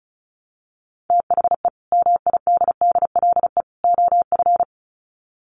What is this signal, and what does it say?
Morse code, heard as a keyed tone, about 34 wpm, 700 Hz. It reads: THE MIDDLE OF
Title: Sets of 3 Words 35wpm